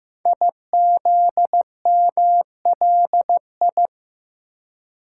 Morse code, heard as a keyed tone, about 15 wpm, 700 Hz. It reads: IZMLI
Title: Code Groups - 5 Letters - 15wpm